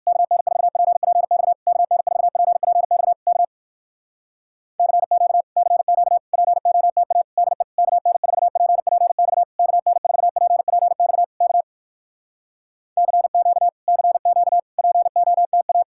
Morse code, heard as a keyed tone, about 59 wpm, 700 Hz. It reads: KN4PPX KN4PPX K CQ CQ POTA DE KN4PPX KN4PPX K CQ CQ POTA